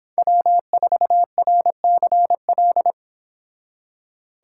Morse code, heard as a keyed tone, about 30 wpm, 700 Hz. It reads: W4RCL